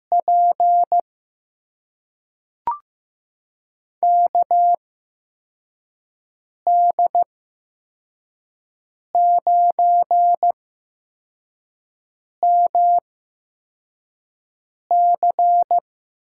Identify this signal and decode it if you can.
Morse code, heard as a keyed tone, about 15 wpm, spaced as for 4 wpm, 700 Hz. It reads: P KD9MC
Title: USA Call Signs - 5 Characters - 8x Character Spacing - 15wpm